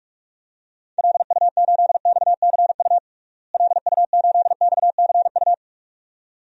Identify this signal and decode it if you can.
Morse code, heard as a keyed tone, about 46 wpm, 700 Hz. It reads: LU8XCU LU8XCU
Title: DXCC-100 Call Signs - 6 Characters - Repeated 2x - 45wpm